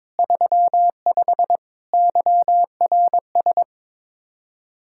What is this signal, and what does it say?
Morse code, heard as a keyed tone, about 21 wpm, 700 Hz. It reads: 35 YRS
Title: Practice QSO - Exchange 3 - More Info - 22wpm